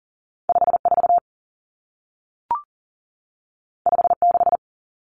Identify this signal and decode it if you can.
Morse code, heard as a keyed tone, about 40 wpm, 700 Hz. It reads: 54 56
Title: Common Cut Numbers - 2-Digit 40wpm